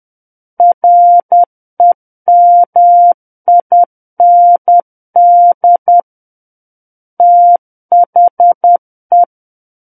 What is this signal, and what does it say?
Morse code, heard as a keyed tone, about 10 wpm, 700 Hz. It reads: REMIND THE
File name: Book One / 01-Chapter 1 - 10 WPM